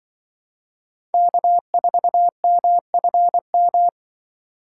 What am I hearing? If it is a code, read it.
Morse code, heard as a keyed tone, about 23 wpm, 700 Hz. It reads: K4MFM
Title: Speed Racing - USA 1x3 Call Signs 20wpm